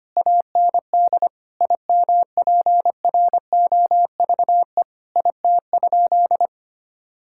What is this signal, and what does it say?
Morse code, heard as a keyed tone, about 25 wpm, 700 Hz. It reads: AND IMPROVE IT?
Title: Sentences from Top 1000 Words - Review Part 5 - 25wpm